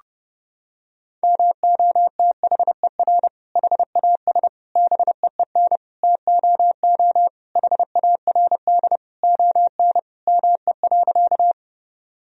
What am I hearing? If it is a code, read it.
Morse code, heard as a keyed tone, about 30 wpm, 700 Hz. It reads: MOTHER HAS BEEN TOO HARD ON ME.
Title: Sentences from Top 1000 Words - Review Part 1 - 30wpm